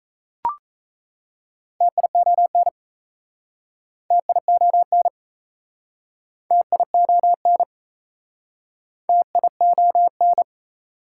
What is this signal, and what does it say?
Morse code, heard as a keyed tone, about 38 wpm, 690 Hz. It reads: TION TION TION TION